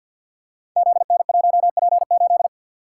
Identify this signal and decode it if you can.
Morse code, heard as a keyed tone, about 46 wpm, 700 Hz. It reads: ZN1P8
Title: Code Groups - 5 Letter:Numbers - 50wpm